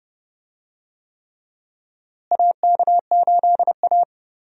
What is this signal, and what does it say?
Morse code, heard as a keyed tone, about 30 wpm, 700 Hz. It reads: AK8A
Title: USA Call Signs - 4 Characters - Repeated 3x - 30wpm